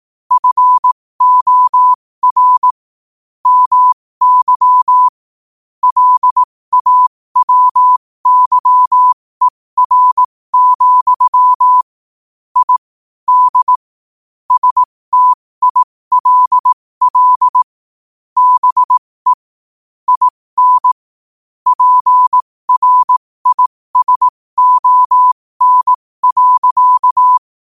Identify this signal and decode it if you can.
Morse code, heard as a keyed tone, about 18 wpm, spaced as for 15 wpm, 1000 Hz. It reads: FOR MY LAWYER, I D STILL BE IN PRISON.
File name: qotdfiles / 15wpm